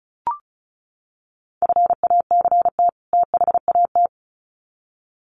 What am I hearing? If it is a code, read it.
Morse code, heard as a keyed tone, about 34 wpm, 700 Hz. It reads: FACT THAT
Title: Sets of 2 Words 35wpm